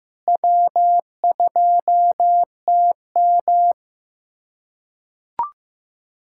Morse code, heard as a keyed tone, about 15 wpm, 700 Hz. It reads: W2TM
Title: USA Callsign DE Callsign - 4 Characters - 15wpm